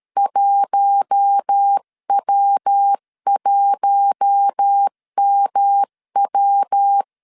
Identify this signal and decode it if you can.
Morse code, heard as a keyed tone, about 13 wpm, 800 Hz. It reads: 1W1MW
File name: CW / DET / 1 6 1500C S C 12wpm 3533